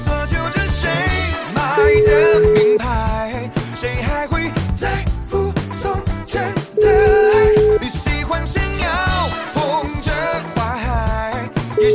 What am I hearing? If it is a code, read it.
Ringback tone (Europe, 425 Hz), with noise.